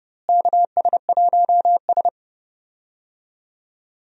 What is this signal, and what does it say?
Morse code, heard as a keyed tone, about 30 wpm, 700 Hz. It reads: KS1S
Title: USA Call Signs - 4 Characters - Repeated 2x - 30wpm